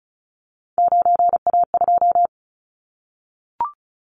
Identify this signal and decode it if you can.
Morse code, heard as a keyed tone, about 34 wpm, 700 Hz. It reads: 9A2
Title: Callsign Prefixes for Top 25 DXCC - Repeated 3x - 35wpm